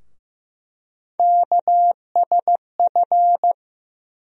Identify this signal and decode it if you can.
Morse code, heard as a keyed tone, about 15 wpm, 700 Hz. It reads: KSF
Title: USA Call Sign Postfixes - 3 Letters 15wpm